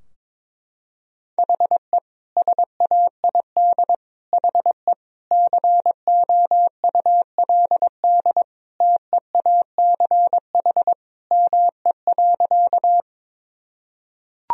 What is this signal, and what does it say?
Morse code, heard as a keyed tone, about 21 wpm, 700 Hz. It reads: HE SAID HE COULD TEACH ME.
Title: Sentences from Top 1000 Words - Review Part 2 - 22wpm